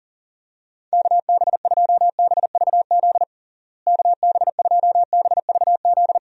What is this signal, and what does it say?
Morse code, heard as a keyed tone, about 37 wpm, 700 Hz. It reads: KB2BVZ KB2BVZ
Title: USA Call Signs - 6 Characters - Repeated 2x - 40wpm